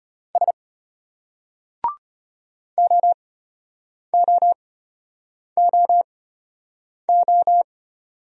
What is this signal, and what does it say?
Morse code, heard as a keyed tone, about 34 wpm, spaced as for 8 wpm, 700 Hz. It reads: S OOOO